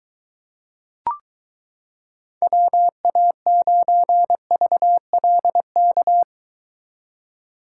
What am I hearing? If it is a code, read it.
Morse code, heard as a keyed tone, about 23 wpm, 700 Hz. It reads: WA9VLK